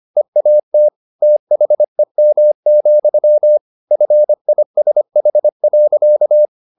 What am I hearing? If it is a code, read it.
Morse code, heard as a keyed tone, about 25 wpm, 600 Hz. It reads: EAT THEM, FISH.
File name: the old man and the sea part 1 25wpm